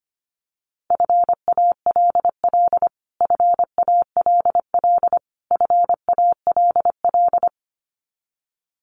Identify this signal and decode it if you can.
Morse code, heard as a keyed tone, about 25 wpm, 700 Hz. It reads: FALL FALL FALL
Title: Top 500 Words Repeated 25wpm